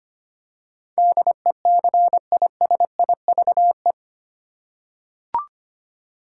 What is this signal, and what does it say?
Morse code, heard as a keyed tone, about 25 wpm, 700 Hz. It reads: DECISIVE
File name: 8 Letter Words - First 3 Letters Given - 25wpm